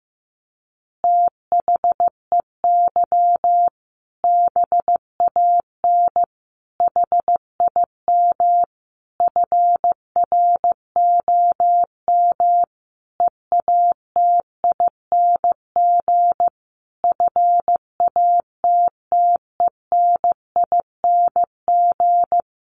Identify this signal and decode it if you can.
Morse code, heard as a keyed tone, about 15 wpm, 700 Hz. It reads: THEY BAN HIM FROM EATING FATTENING
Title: Easy Phrases 15wpm